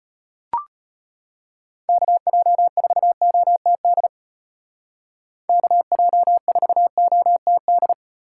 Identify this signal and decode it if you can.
Morse code, heard as a keyed tone, about 30 wpm, 700 Hz. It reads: KJ4OTD KJ4OTD
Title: Speed Racing - USA Call Signs 25wpm